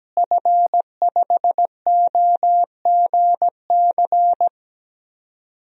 Morse code, heard as a keyed tone, about 17 wpm, 700 Hz. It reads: F5OGC
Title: DXCC-100 Call Signs - 5 Characters - Repeated 2x - 17wpm